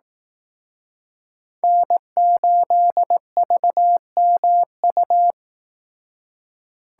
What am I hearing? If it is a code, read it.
Morse code, heard as a keyed tone, about 18 wpm, 700 Hz. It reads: N8VMU
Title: Speed Racing - USA 1x3 Call Signs 15wpm